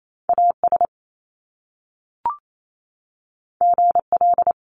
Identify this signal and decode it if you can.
Morse code, heard as a keyed tone, about 28 wpm, 700 Hz. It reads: AS GL
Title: Top Two Letter Combinations in Words 28wpm